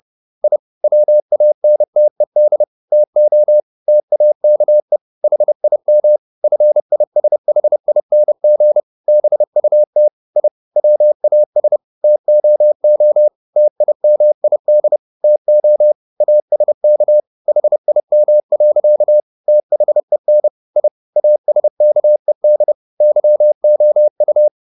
Morse code, heard as a keyed tone, about 30 wpm, 600 Hz. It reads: I WANTED TO TAKE HIM FISHING BUT I WAS TOO TIMID TO ASK HIM. THEN I ASKED YOU
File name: the old man and the sea / the old man and the sea 1 30wpm